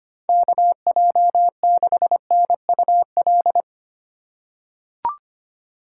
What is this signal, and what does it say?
Morse code, heard as a keyed tone, about 25 wpm, 700 Hz. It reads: KJ6NUL
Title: USA Call Signs - 6 Characters - 3x Character Spacing - 25wpm